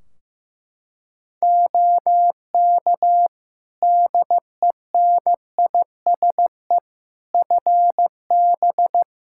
Morse code, heard as a keyed tone, about 16 wpm, 700 Hz. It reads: OK DENISE FB
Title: Practice QSO - Exchange 2 - Info - 15wpm